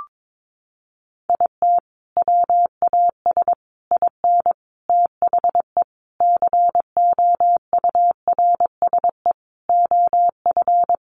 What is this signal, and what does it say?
Morse code, heard as a keyed tone, about 22 wpm, 700 Hz. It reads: IT WAS IN THE COURSE OF
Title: Sentences from Top 1400 Words 22wpm